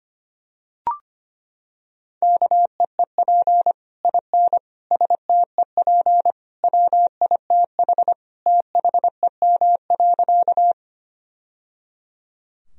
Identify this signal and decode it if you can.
Morse code, heard as a keyed tone, about 25 wpm, 700 Hz. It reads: KEEP IN STEP WITH THEM.
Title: Sentences from Top 1000 Words - Review Part 2 - 25wpm